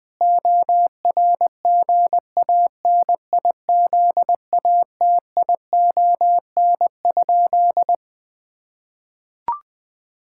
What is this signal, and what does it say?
Morse code, heard as a keyed tone, about 20 wpm, 700 Hz. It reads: ORGANIZATION?
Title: Sentences from Top 1000 Words - Review Part 5 - 20wpm